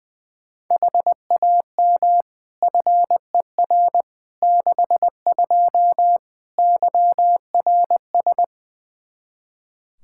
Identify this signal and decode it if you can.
Morse code, heard as a keyed tone, about 21 wpm, 700 Hz. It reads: HAM FER 62 YRS
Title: K1USN SST QSO Practice 20wpm